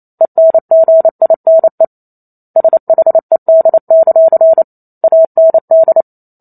Morse code, heard as a keyed tone, about 28 wpm, 650 Hz. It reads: ENGINE SHED; AND